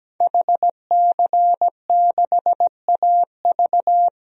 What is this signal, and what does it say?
Morse code, heard as a keyed tone, about 17 wpm, 700 Hz. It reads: HC6AV
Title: DXCC-100 Call Signs - 5 Characters - Repeated 2x - 17wpm